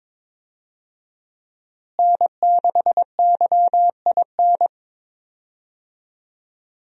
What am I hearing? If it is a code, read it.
Morse code, heard as a keyed tone, about 22 wpm, 700 Hz. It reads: N6YIN
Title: USA Call Signs - 5 Characters - Repeated 2x - 22wpm